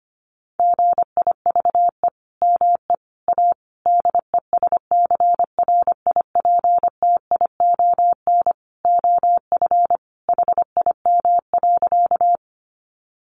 Morse code, heard as a keyed tone, about 25 wpm, 700 Hz. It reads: GIVE ME A DESCRIPTION OF HIM.
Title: Sentences from Top 1300 Words 25wpm